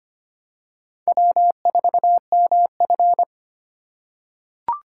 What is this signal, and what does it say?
Morse code, heard as a keyed tone, about 25 wpm, 700 Hz. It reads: W4MF
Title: USA Call Signs - 6x Character Spacing - 25wpm